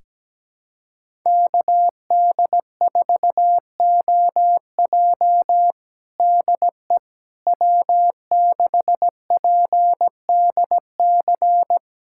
Morse code, heard as a keyed tone, about 17 wpm, 700 Hz. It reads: KD4OJ DE W6PDC
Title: USA Callsign DE Callsign - 5 Characters - 17wpm